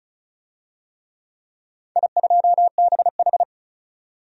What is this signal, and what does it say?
Morse code, heard as a keyed tone, about 36 wpm, 700 Hz. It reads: I2BH
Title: DXCC-100 Call Signs - 4 Characters - 35wpm